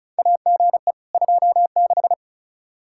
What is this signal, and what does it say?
Morse code, heard as a keyed tone, about 34 wpm, 700 Hz. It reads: AGE 26
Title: Practice QSO - Exchange 3 - More Info - 35wpm